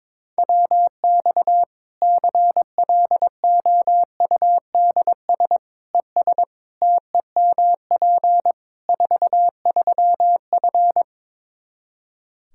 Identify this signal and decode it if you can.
Morse code, heard as a keyed tone, about 22 wpm, 700 Hz. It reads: WX CLOUDS ES TEMP 43F
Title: K1USN SST QSO Practice 22wpm